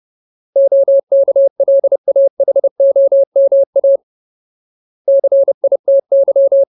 Morse code, heard as a keyed tone, about 30 wpm, 550 Hz. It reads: OKLAHOMA CITY
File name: Ultimate Word List / 30WPM Ultimate